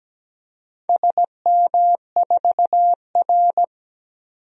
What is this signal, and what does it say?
Morse code, heard as a keyed tone, about 17 wpm, 700 Hz. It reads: SM4R
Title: DXCC-100 Call Signs - 4 Characters - 17wpm